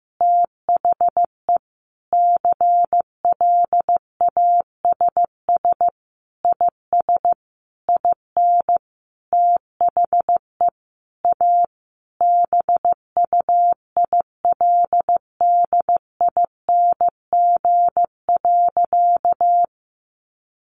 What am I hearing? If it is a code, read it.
Morse code, heard as a keyed tone, about 15 wpm, 700 Hz. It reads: THE CLASS IS IN THE A BUILDING.
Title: Additional Easy Phrases 15wpm